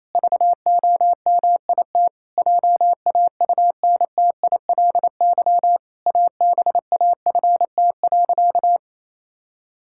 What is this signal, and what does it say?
Morse code, heard as a keyed tone, about 28 wpm, 700 Hz. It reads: VOMIT JAUNTILY ABAFT.
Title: Pangram Challenge v2 - 28wpm